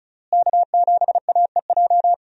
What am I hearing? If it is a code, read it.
Morse code, heard as a keyed tone, about 34 wpm, 700 Hz. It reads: K7AEJ
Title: USA Call Signs - 5 Characters - Repeated 2x - 35wpm